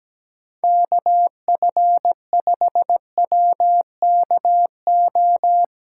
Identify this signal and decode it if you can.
Morse code, heard as a keyed tone, about 17 wpm, 700 Hz. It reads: KF5WKO
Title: USA Callsign DE Callsign 17wpm